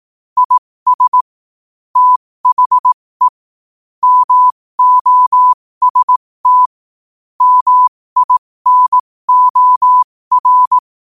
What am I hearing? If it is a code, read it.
Morse code, heard as a keyed tone, about 18 wpm, spaced as for 15 wpm, 1000 Hz. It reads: IS THE MOST MINOR
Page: New quotes every day in morse code at 15 Words per minute.